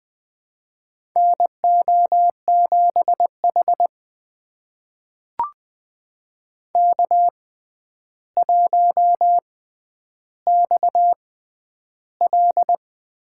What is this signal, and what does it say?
Morse code, heard as a keyed tone, about 20 wpm, spaced as for 10 wpm, 700 Hz. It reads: NO7H K1XL
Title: USA Call Signs - 4 Characters - 6x Character Spacing - 20wpm